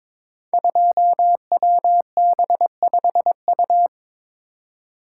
Morse code, heard as a keyed tone, about 22 wpm, 700 Hz. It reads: 2WB5U
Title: Code Groups - 5 Letter:Numbers - 22wpm